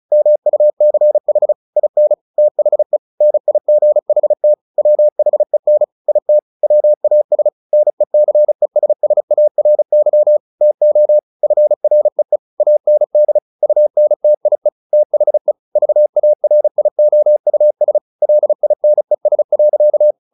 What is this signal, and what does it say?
Morse code, heard as a keyed tone, about 34 wpm, 600 Hz. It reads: MUCH IN THE NIGHT WHEN IT WAS NECESSARY TO FREE AND UNTIE THE VARIOUS LINES.
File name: the old man and the sea / the old man and the sea 1 35wpm